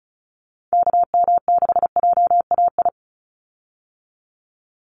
Morse code, heard as a keyed tone, about 36 wpm, 700 Hz. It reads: KM6JAI
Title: USA Call Signs 35wpm